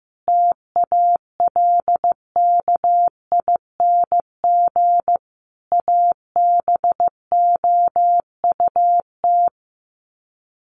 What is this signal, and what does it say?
Morse code, heard as a keyed tone, about 15 wpm, 700 Hz. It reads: TALKING ABOUT
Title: Sets of 5 Words 15wpm